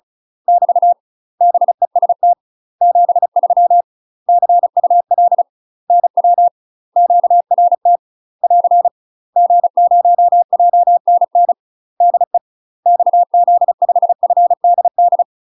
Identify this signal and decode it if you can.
Morse code, heard as a keyed tone, about 36 wpm, 700 Hz. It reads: <BT> BEST 73 CUL NW QRT <AR> G0JNN DE XZ5FDD
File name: UK MASTER QSOs TRAINING / 35WPM / UK MASTER QSOs 0007